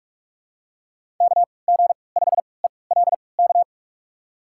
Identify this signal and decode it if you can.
Morse code, heard as a keyed tone, about 46 wpm, spaced as for 26 wpm, 700 Hz. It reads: KG5ELK
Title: USA Call Signs - 6 Characters - 3x Character Spacing - 45wpm